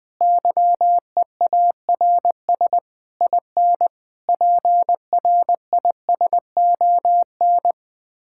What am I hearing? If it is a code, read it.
Morse code, heard as a keyed tone, about 20 wpm, 700 Hz. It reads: YEARS IN PRISON
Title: Sets of 6 Words - 20wpm